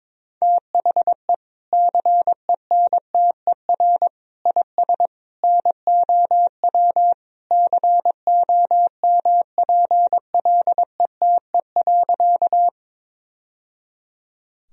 Morse code, heard as a keyed tone, about 22 wpm, 700 Hz. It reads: THE CENTER IS NOW COMPLETE.
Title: Sentences from Top 1000 Words - Review Part 2 - 22wpm